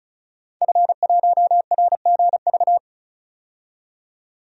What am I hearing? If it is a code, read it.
Morse code, heard as a keyed tone, about 36 wpm, 700 Hz. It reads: F1RGV
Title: DXCC-100 Call Signs - 5 Characters - 35wpm